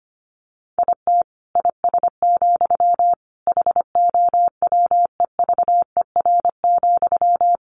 Morse code, heard as a keyed tone, about 25 wpm, 700 Hz. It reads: IT IS, HOWEVER,
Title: Sentences from Top 1400 Words 25wpm